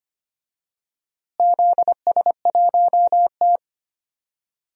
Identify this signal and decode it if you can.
Morse code, heard as a keyed tone, about 25 wpm, 700 Hz. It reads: ZS1T
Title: DXCC-100 Call Signs - 4 Characters - Repeated 3x - 25wpm